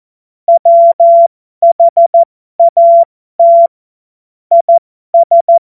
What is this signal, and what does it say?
Morse code, heard as a keyed tone, about 14 wpm, spaced as for 12 wpm, 680 Hz. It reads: WHAT IS
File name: Morsekurs / Witze